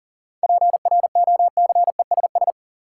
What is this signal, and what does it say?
Morse code, heard as a keyed tone, about 40 wpm, 700 Hz. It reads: PROCESS
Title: Sentences from Top 1000 Words - Review Part 3 - New Word Drill - 40wpm